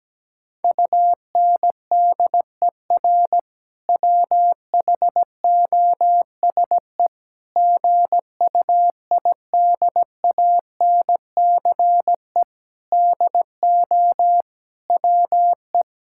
Morse code, heard as a keyed tone, about 17 wpm, 700 Hz. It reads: UNDER WHOSE GUIDANCE DO WE
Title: Sentences from Top 1600 Words 17wpm